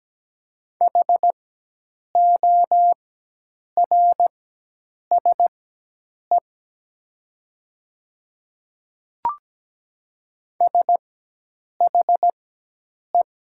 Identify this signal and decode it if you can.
Morse code, heard as a keyed tone, about 17 wpm, spaced as for 8 wpm, 700 Hz. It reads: HORSE SHE
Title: 4x Character Spacing - Top 300 Words - 17wpm